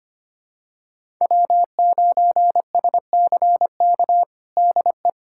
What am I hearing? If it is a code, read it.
Morse code, heard as a keyed tone, about 25 wpm, 700 Hz. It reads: W9SCK DE
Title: USA Callsign DE Callsign - 5 Characters - 25wpm